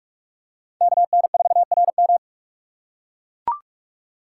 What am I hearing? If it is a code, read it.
Morse code, heard as a keyed tone, about 42 wpm, 700 Hz. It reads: KN4RM
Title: USA Call Signs - 5 Characters - 3x Character Spacing - 45wpm